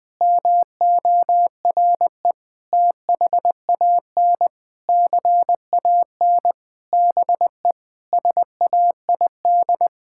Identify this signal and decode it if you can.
Morse code, heard as a keyed tone, about 20 wpm, 700 Hz. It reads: MORE THAN CAN BE SAID
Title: Sentences with 8-Word Phrases - 20wpm